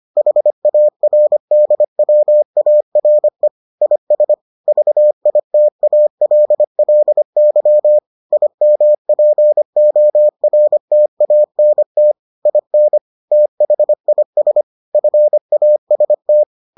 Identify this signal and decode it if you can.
Morse code, heard as a keyed tone, about 25 wpm, 600 Hz. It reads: HARDWARE IS VITALLY IMPORTANT IN THIS FAST